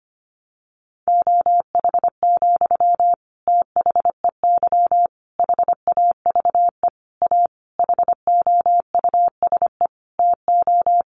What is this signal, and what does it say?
Morse code, heard as a keyed tone, about 25 wpm, 700 Hz. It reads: OH, THEY HAVE A HOUSE TO
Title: Sentences from Top 800 Words 25wpm